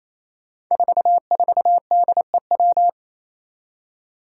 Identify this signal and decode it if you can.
Morse code, heard as a keyed tone, about 28 wpm, 700 Hz. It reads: 44DEW
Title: Five Challenging Letter-Number Combinations 28wpm